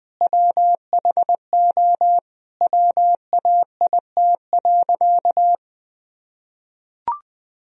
Morse code, heard as a keyed tone, about 20 wpm, 700 Hz. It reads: WHO WAIT.
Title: Sentences from Top 1000 Words - Review Part 2 - 20wpm